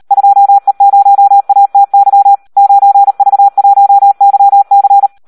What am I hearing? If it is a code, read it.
Morse code, heard as a keyed tone, about 38 wpm, 800 Hz. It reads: 2E0ATY 9V1YC
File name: Audio practica CW / SD 1 38